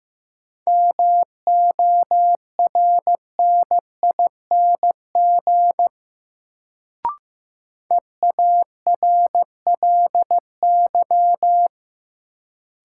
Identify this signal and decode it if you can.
Morse code, heard as a keyed tone, about 15 wpm, 700 Hz. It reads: MORNING EARLY
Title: Sets of 3 Words - New Word Drill - 15wpm